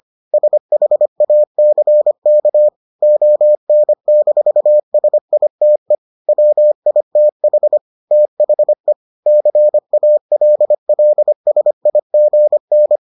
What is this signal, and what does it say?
Morse code, heard as a keyed tone, about 25 wpm, 600 Hz. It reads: SHACK ON-SITE WITH THE CALLSIGN